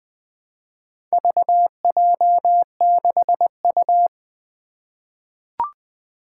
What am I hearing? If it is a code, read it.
Morse code, heard as a keyed tone, about 20 wpm, 700 Hz. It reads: VJ6U
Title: Four Challenging Letter-Number Combinations 20wpm